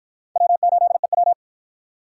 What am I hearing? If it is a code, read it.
Morse code, heard as a keyed tone, about 53 wpm, 700 Hz. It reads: W8EW